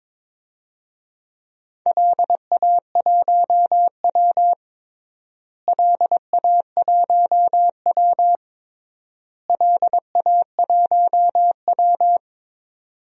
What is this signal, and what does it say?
Morse code, heard as a keyed tone, about 22 wpm, 700 Hz. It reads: LA1W LA1W LA1W
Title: DXCC-100 Call Signs - 4 Characters - Repeated 3x - 22wpm